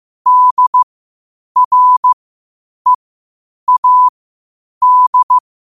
Morse code, heard as a keyed tone, about 15 wpm, spaced as for 8 wpm, 1000 Hz. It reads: DREAD
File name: qotdfiles / 8wpm